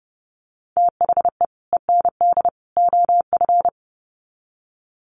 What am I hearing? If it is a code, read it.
Morse code, heard as a keyed tone, about 30 wpm, 700 Hz. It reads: THE END OF
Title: Sets of 3 Words 30wpm